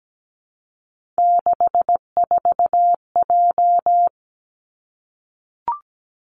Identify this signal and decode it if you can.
Morse code, heard as a keyed tone, about 17 wpm, 700 Hz. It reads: 64J
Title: Three Challenging Letter-Number Combinations 17wpm